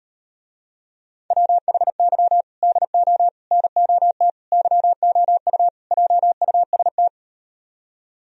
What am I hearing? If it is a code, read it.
Morse code, heard as a keyed tone, about 38 wpm, 700 Hz. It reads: WHY DO NOT YOU JUST